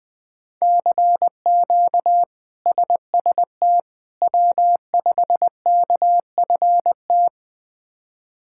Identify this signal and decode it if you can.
Morse code, heard as a keyed tone, about 20 wpm, 700 Hz. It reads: CQ SST W5KFT
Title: K1USN SST QSO Practice - 20wpm